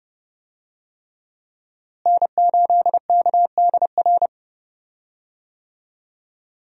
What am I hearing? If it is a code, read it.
Morse code, heard as a keyed tone, about 30 wpm, 700 Hz. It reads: N8KDR
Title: USA Call Signs - 5 Characters - Repeated 3x - 30wpm